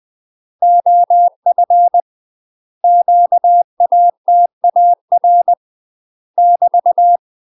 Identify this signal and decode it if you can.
Morse code, heard as a keyed tone, about 20 wpm, 700 Hz. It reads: OF QATAR <BT>
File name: UK MASTER QSOs TRAINING / 20WPM / UK MASTER QSOs 0034